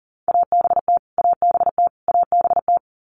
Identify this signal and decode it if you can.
Morse code, heard as a keyed tone, about 40 wpm, 700 Hz. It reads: ABT ABT ABT
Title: Abbreviations Repeated 3x - 40wpm